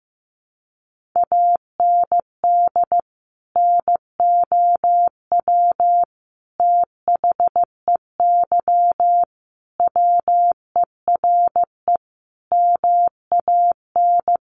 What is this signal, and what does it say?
Morse code, heard as a keyed tone, about 15 wpm, 700 Hz. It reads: AND NOW THEY WERE MAN
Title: Sentences from Top 1100 Words 15wpm